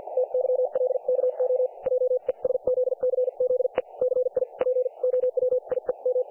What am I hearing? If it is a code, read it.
Morse code, heard as a keyed tone, about 50 wpm, spaced as for 41 wpm, 500 Hz. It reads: TYLKO JESZCZE CI POWIEM
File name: dlazaawansowanych / nagraniazpasma / websdr recording 3564.7kHz